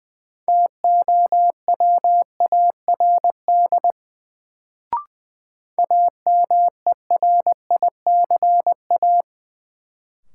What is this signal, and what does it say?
Morse code, heard as a keyed tone, about 21 wpm, 700 Hz. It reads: TOWARD AMERICA
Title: Sets of 5 Words - New Word Drill - 20wpm